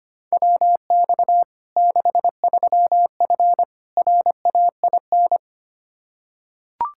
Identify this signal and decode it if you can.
Morse code, heard as a keyed tone, about 25 wpm, 700 Hz. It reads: WX 63F RAIN
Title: Practice QSO - Exchange 2 - Info - 25wpm